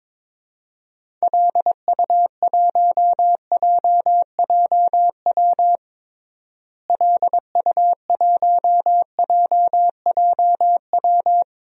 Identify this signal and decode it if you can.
Morse code, heard as a keyed tone, about 21 wpm, 700 Hz. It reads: LU1JJW LU1JJW
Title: DXCC-100 Call Signs - 6 Characters - Repeated 3x - 22wpm